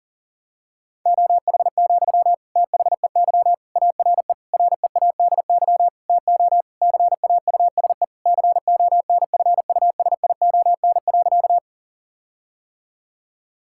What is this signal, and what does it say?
Morse code, heard as a keyed tone, about 40 wpm, 700 Hz. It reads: OH, THEY ARE READY TO CAUSE CONFUSION.
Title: Sentences from Top 1600 Words 40wpm